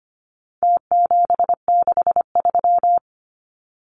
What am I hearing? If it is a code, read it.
Morse code, heard as a keyed tone, about 25 wpm, 700 Hz. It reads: T763
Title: Numbers - Contest Style - 4-Digit 25wpm